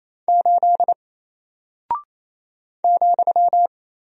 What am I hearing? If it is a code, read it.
Morse code, heard as a keyed tone, about 27 wpm, 700 Hz. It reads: 8 ,
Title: Single Character (Letter:Number:Punctuation:Prosign) Alt 28wpm